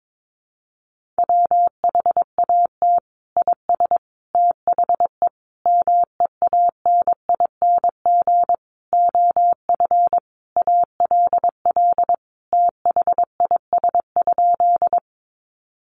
Morse code, heard as a keyed tone, about 22 wpm, 700 Hz. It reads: WHAT IS THE MEANING OF ALL THIS?
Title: Sentences from Top 900 Words 22wpm